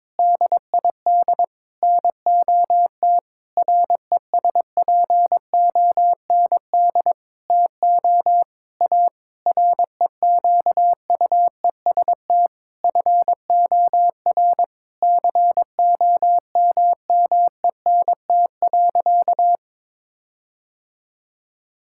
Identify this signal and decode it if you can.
Morse code, heard as a keyed tone, about 22 wpm, 700 Hz. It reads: DID NOT RESPOND TO A REQUEST FOR COMMENT.
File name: practice / 22wpm / Sentences with 8-Word Phrases - 22wpm